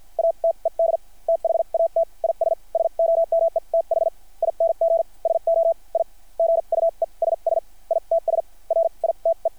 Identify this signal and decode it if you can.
Morse code, heard as a keyed tone, about 50 wpm, spaced as for 39 wpm, 660 Hz. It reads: ATED THAT IS SOMETHING SO I GUESS ITS RITE
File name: archived / CRSnet